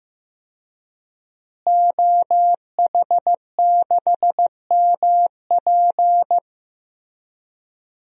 Morse code, heard as a keyed tone, about 15 wpm, 700 Hz. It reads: OH6MP
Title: DXCC-100 Call Signs - 5 Characters - Repeated 3x - 15wpm